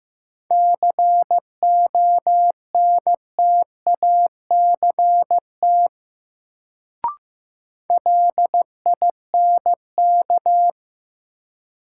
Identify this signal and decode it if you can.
Morse code, heard as a keyed tone, about 15 wpm, 700 Hz. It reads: CONTACT LINK
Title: Sentences from Top 1000 Words - Review Part 7 - New Word Drill - 15wpm